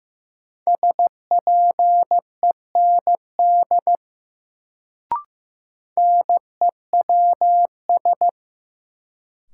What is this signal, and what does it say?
Morse code, heard as a keyed tone, about 15 wpm, 700 Hz. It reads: SPEND NEWS
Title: Sentences from Top 800 Words - New Word Drill - 15wpm